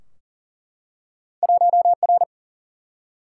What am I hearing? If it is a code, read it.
Morse code, heard as a keyed tone, about 39 wpm, 700 Hz. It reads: 1R
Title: Single-Letter Single-Number 40wpm